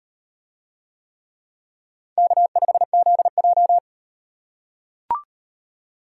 Morse code, heard as a keyed tone, about 38 wpm, 700 Hz. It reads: K5ZJ